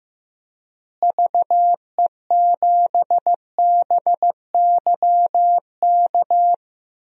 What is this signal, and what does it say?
Morse code, heard as a keyed tone, about 15 wpm, 700 Hz. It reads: VE7BYK
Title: DXCC-100 Call Signs - 6 Characters - Repeated 2x - 15wpm